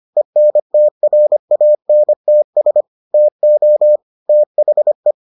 Morse code, heard as a keyed tone, about 25 wpm, 600 Hz. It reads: ENTRANTS TO THE